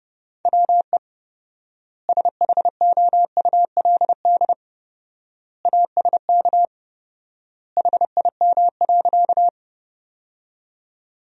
Morse code, heard as a keyed tone, about 32 wpm, 700 Hz. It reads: WE SHOULD ASK HIM.
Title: 4x Word-Spacing - Sentences from Top 300 Words - 30wpm